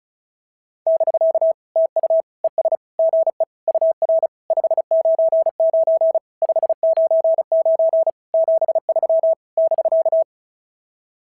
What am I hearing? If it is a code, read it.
Morse code, heard as a keyed tone, about 33 wpm, 650 Hz. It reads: <BK> TU ES GE UR 599 599 73 <BK>
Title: SOTA QSO Practice 35wpm